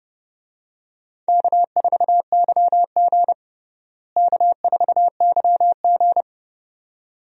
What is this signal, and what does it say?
Morse code, heard as a keyed tone, about 32 wpm, 700 Hz. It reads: K4YG K4YG
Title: USA Call Signs - 4 Characters - Repeated 2x - 30wpm